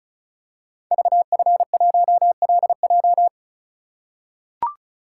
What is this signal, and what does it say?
Morse code, heard as a keyed tone, about 36 wpm, 700 Hz. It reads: VF1LJ
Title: Code Groups - 5 Letter:Numbers - 35wpm